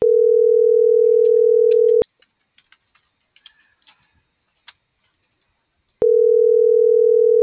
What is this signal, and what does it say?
Ringback tone (North America), 440 + 480 Hz.